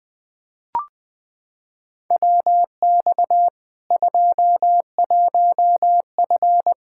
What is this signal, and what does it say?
Morse code, heard as a keyed tone, about 20 wpm, 700 Hz. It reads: WX 21F
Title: K1USN SST QSO Practice 20wpm